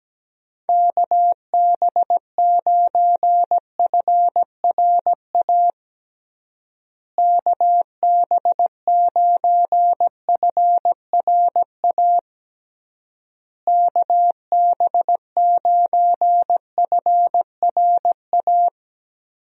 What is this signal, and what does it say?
Morse code, heard as a keyed tone, about 17 wpm, 700 Hz. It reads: KB9FRA KB9FRA KB9FRA
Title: USA Call Signs - 6 Characters - Repeated 3x - 17wpm